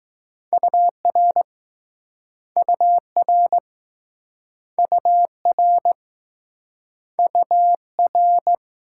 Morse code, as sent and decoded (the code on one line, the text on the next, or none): ..- .-. / ..- .-. / ..- .-. / ..- .-.
UR UR UR UR